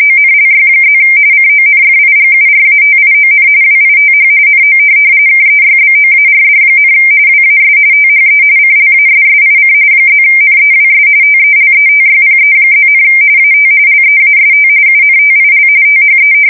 RTTY, about 45.45 baud. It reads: ONLY+ CONFIGURATION WITH NO KEYBOARD, WAS THE CLASSIC +NEWS TELETYPE+, AT LEAST UNTIL THE
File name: fsk170 / teletype-corp